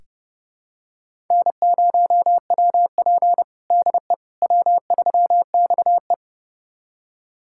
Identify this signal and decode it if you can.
Morse code, heard as a keyed tone, about 30 wpm, 700 Hz. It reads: N0WP DE W3XE
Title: USA Callsign DE Callsign - 4 Characters - 30wpm